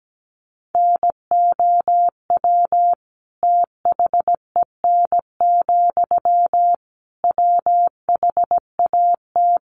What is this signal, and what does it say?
Morse code, heard as a keyed tone, about 17 wpm, 700 Hz. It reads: NOW THEN, WHAT
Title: Sentences from Top 1100 Words 17wpm